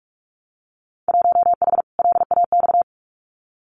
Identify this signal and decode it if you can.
Morse code, heard as a keyed tone, about 46 wpm, 700 Hz. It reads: 1H LAX
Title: Winter Field Day Exchanges 45wpm